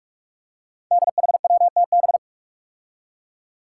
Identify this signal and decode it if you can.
Morse code, heard as a keyed tone, about 45 wpm, 700 Hz. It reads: DHWTB
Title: Five Challenging Letter-Number Combinations 45wpm